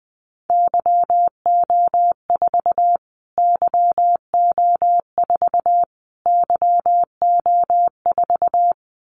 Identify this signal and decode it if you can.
Morse code, heard as a keyed tone, about 20 wpm, 700 Hz. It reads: YO4 YO4 YO4
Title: Callsign Prefixes for Top 25 DXCC - Repeated 3x - 20wpm